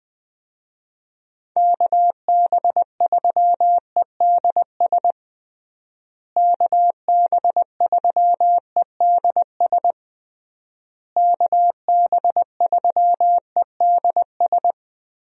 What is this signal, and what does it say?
Morse code, heard as a keyed tone, about 21 wpm, 700 Hz. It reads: KB3EDS KB3EDS KB3EDS
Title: USA Call Signs - 6 Characters - Repeated 3x - 20wpm